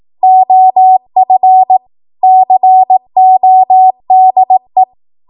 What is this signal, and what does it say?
Morse code, heard as a keyed tone, about 18 wpm, 750 Hz. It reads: OF CODE
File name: Archive / 18 WPM / 18WPM